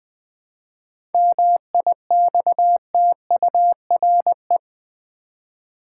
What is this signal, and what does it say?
Morse code, heard as a keyed tone, about 19 wpm, 700 Hz. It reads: MIXTURE
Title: Seven Letter Words - 20wpm